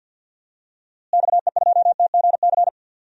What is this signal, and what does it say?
Morse code, heard as a keyed tone, about 52 wpm, 700 Hz. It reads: XE2TGC
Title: DXCC-100 Call Signs - 6 Characters - Repeated 3x - 50wpm